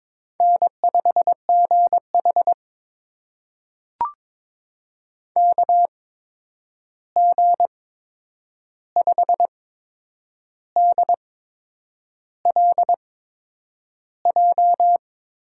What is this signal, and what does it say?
Morse code, heard as a keyed tone, about 22 wpm, spaced as for 8 wpm, 700 Hz. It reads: N5GH KG5DLJ
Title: USA Call Signs - 8x Character Spacing - 22wpm